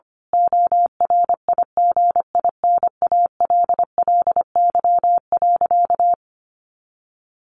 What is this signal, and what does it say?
Morse code, heard as a keyed tone, about 25 wpm, 700 Hz. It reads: ORIGINALLY.
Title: A Sound Of Thunder 25wpm